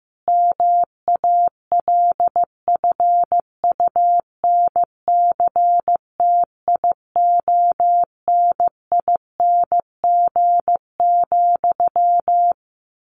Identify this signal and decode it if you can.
Morse code, heard as a keyed tone, about 15 wpm, 700 Hz. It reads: MALFUNCTIONING,
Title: A Sound Of Thunder 15wpm